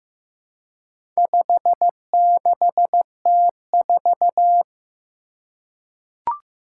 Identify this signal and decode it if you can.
Morse code, heard as a keyed tone, about 15 wpm, 700 Hz. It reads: 56T4
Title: Four Challenging Letter-Number Combinations 15wpm